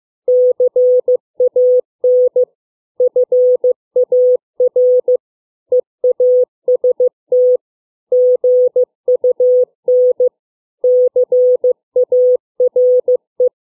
Morse code, heard as a keyed tone, about 15 wpm, 500 Hz. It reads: CAN FAR EAST GUN CARE